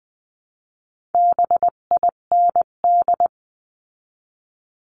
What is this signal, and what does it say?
Morse code, heard as a keyed tone, about 20 wpm, 700 Hz. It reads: BIND
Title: 258 Lesson 42c - Words 16-20wpm